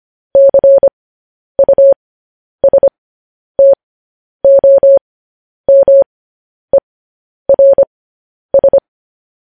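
Morse code, heard as a keyed tone, about 25 wpm, spaced as for 10 wpm, 550 Hz. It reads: CUSTOMERS
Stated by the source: Morse code news headlines sent as a reading transcription, preceded by vvv.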